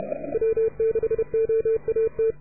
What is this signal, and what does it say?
Morse code, heard as a keyed tone, about 31 wpm, 440 Hz. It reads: W6OAT